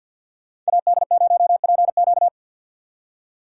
Morse code, heard as a keyed tone, about 65 wpm, spaced as for 57 wpm, 700 Hz. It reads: AD0PQ
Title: Speed Racing - USA Call Signs 50wpm